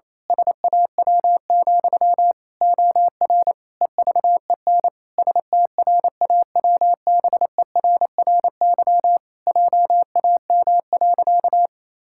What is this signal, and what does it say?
Morse code, heard as a keyed tone, about 28 wpm, 700 Hz. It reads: SAW, OR EVEN STRAWBERRY JAM.
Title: Pangram Challenge v2 - 28wpm